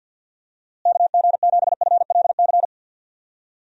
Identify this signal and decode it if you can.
Morse code, heard as a keyed tone, about 48 wpm, 700 Hz. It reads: KG7FLC
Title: USA Call Signs - 6 Characters - Repeated 2x - 50wpm